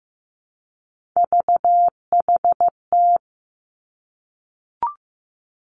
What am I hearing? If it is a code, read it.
Morse code, heard as a keyed tone, about 15 wpm, 700 Hz. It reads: VHT
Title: Three Challenging Letter-Number Combinations 15wpm